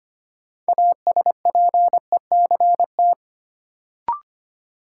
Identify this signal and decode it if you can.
Morse code, heard as a keyed tone, about 25 wpm, 700 Hz. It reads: ASPECT
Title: Sentences from Top 1300 Words - New Word Drill - 25wpm